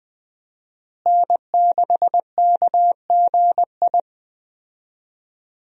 Morse code, heard as a keyed tone, about 20 wpm, 700 Hz. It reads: N6KGI
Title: USA Call Signs - 5 Characters - Repeated 2x - 20wpm